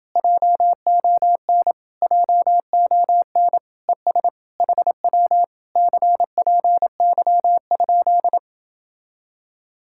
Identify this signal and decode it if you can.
Morse code, heard as a keyed tone, about 29 wpm, 700 Hz. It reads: JON JON ES HW CPY?